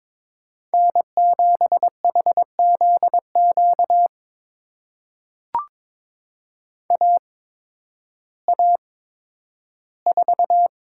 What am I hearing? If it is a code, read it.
Morse code, heard as a keyed tone, about 22 wpm, spaced as for 12 wpm, 700 Hz. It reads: N7HZQ AA4
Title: USA Call Signs - 5 Characters - 8x Character Spacing - 22wpm